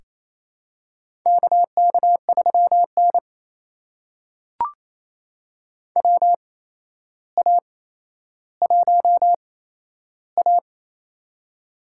Morse code, heard as a keyed tone, about 28 wpm, spaced as for 12 wpm, 700 Hz. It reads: KK3N WA1A
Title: USA Call Signs - 4 Characters - 8x Character Spacing - 28wpm